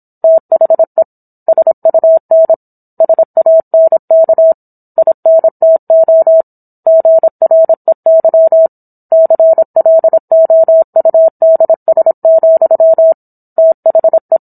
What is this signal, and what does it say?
Morse code, heard as a keyed tone, about 27 wpm, 650 Hz. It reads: THE SUN SANK INTO GREY CLOUDS, THE